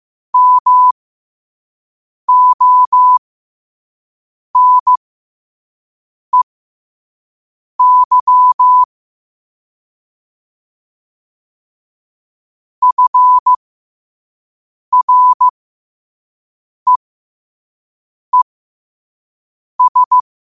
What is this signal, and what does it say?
Morse code, heard as a keyed tone, about 15 wpm, spaced as for 5 wpm, 1000 Hz. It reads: MONEY FREES
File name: qotdfiles / 5wpm